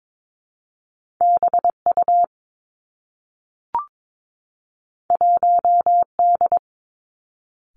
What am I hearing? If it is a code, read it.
Morse code, heard as a keyed tone, about 22 wpm, 700 Hz. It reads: B U 1 D
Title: Two Challenging Letter-Number Combinations 22wpm